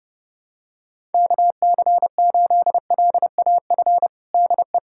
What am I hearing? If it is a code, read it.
Morse code, heard as a keyed tone, about 30 wpm, 700 Hz. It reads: KC8LAF DE
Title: USA Callsign DE Callsign - 6 Characters - 30wpm